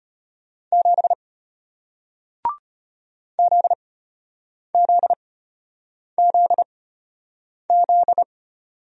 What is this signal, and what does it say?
Morse code, heard as a keyed tone, about 32 wpm, spaced as for 8 wpm, 700 Hz. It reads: 7 ZZZZ